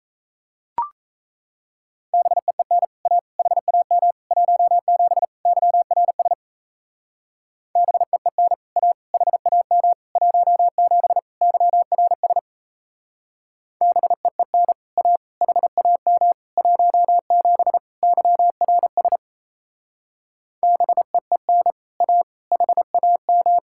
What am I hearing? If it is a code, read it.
Morse code, heard as a keyed tone, about 34 wpm, 700 Hz. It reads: BEEN A HAM 17 YRS BEEN A HAM 17 YRS BEEN A HAM 17 YRS BEEN A HAM
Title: Speed-Racing Common QSO Phrases - 28wpm